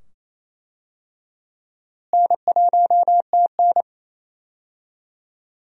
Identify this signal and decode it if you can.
Morse code, heard as a keyed tone, about 28 wpm, 700 Hz. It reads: N1TN
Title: USA Call Signs - 4 Characters - Repeated 2x - 28wpm